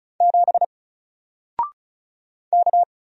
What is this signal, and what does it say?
Morse code, heard as a keyed tone, about 36 wpm, 700 Hz. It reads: 7 K
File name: Single-Letter-Number-35wpm